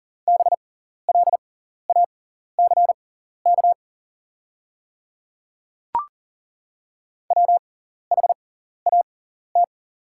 Morse code, heard as a keyed tone, about 40 wpm, spaced as for 14 wpm, 690 Hz. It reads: BLACK WHAT
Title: 6x Character Spacing - Top 500 Words - 40wpm